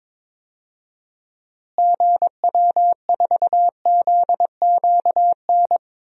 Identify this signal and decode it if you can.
Morse code, heard as a keyed tone, about 22 wpm, 700 Hz. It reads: GW4ZQN
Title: DXCC-100 Call Signs - 6 Characters - Repeated 3x - 22wpm